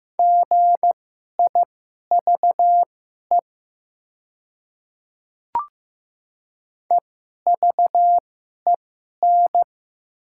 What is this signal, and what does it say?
Morse code, heard as a keyed tone, about 15 wpm, spaced as for 11 wpm, 700 Hz. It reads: GIVE EVEN
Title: 2x Character Spacing - Top 200 Words - 15wpm